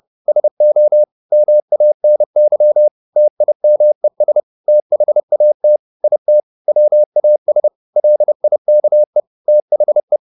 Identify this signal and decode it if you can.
Morse code, heard as a keyed tone, about 30 wpm, 600 Hz. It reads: SO MANY TIMES THAT IT WAS LIKE THE
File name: the old man and the sea / the old man and the sea 1 30wpm